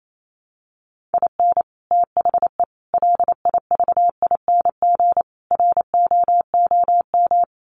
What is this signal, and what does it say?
Morse code, heard as a keyed tone, about 28 wpm, 700 Hz. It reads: IN THE LIVING ROOM
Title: Sets of 4 Words 28wpm